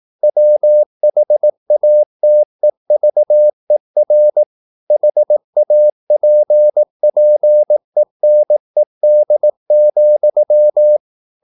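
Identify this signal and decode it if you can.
Morse code, heard as a keyed tone, about 18 wpm, 600 Hz. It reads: WHATEVER HAPPENED,
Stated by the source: Morse practice at 18 wpm, English text, from CW med Gnister.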